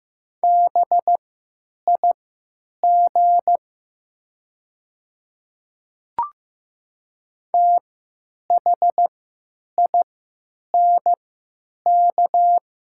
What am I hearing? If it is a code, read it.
Morse code, heard as a keyed tone, about 15 wpm, spaced as for 9 wpm, 700 Hz. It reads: BIG THINK
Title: 3x Character Spacing - Top 200 Words - 15wpm